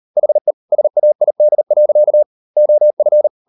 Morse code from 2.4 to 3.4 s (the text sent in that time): OF